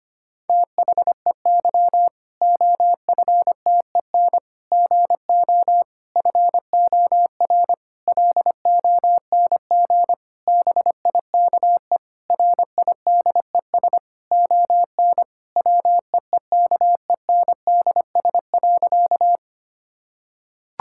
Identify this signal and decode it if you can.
Morse code, heard as a keyed tone, about 25 wpm, 700 Hz. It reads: THEY OFTEN GO FOR LONG BIKE RIDES ON WEEKENDS.
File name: practice / 25wpm / Difficult Sentences with Hints - 25wpm